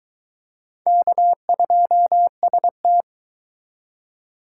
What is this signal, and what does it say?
Morse code, heard as a keyed tone, about 23 wpm, 700 Hz. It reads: K2ST